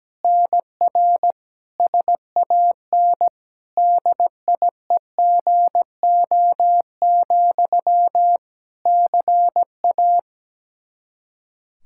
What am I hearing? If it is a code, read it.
Morse code, heard as a keyed tone, about 17 wpm, 700 Hz. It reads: NR SAN DIEGO, CA
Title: Practice QSO - Exchange 1 - Beginning - 17wpm